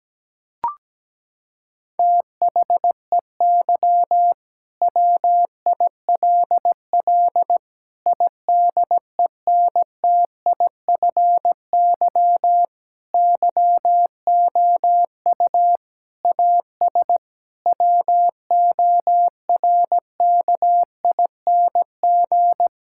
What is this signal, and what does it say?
Morse code, heard as a keyed tone, about 17 wpm, 700 Hz. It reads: THEY WILL IDENTIFY YOU AS WORKING
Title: Sentences from Top 1000 Words - Review Part 6 - 17wpm